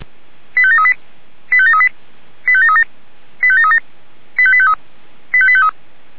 Five-tone selective call (ZVEI), heard as quick runs of five tones, tones of about 80 ms. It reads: 85738 85738 85737 85737 85853 85853
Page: SDR/ZVEI